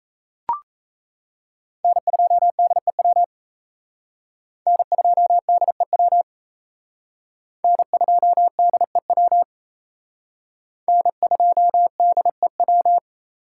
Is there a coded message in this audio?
Yes: Morse code from 1.8 to 13.0 s, heard as a keyed tone, about 36 wpm, 700 Hz. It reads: N2DEW N2DEW N2DEW N2DEW